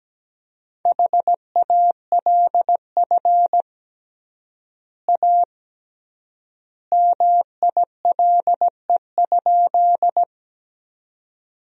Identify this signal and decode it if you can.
Morse code, heard as a keyed tone, about 17 wpm, 700 Hz. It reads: HALF A MILE?
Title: 3x Word-Spacing - Sentences from Top 400 Words - 17wpm